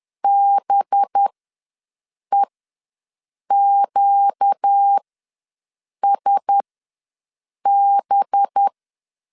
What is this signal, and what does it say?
Morse code, heard as a keyed tone, about 10 wpm, spaced as for 6 wpm, 800 Hz. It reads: BEQSB
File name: LEZIONI / 5 LBVC / 5 LB ws 10wpm 3616